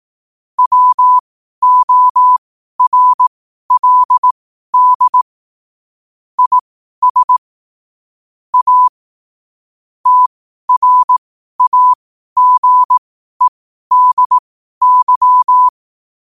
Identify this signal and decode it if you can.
Morse code, heard as a keyed tone, about 18 wpm, spaced as for 12 wpm, 1000 Hz. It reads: WORLD IS A TRAGEDY
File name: qotdfiles / 12wpm